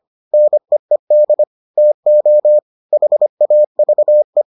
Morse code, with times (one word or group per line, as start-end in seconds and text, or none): 0.3-1.4 NEED
1.8-2.6 TO
2.9-4.4 HAVE